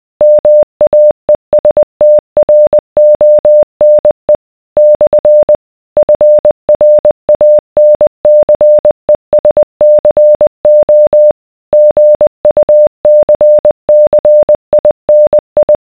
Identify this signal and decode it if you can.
Morse code, heard as a keyed tone, about 20 wpm, 600 Hz. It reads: MAESTRONE / FRANCESCO GUCCINI